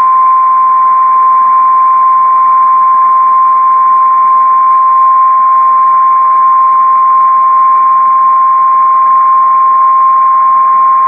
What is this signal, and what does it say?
Emergency Alert System (SAME) tones (the 1050 Hz attention tone). It is heard from 0.7 to 10.2 s.
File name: VEX beacon juan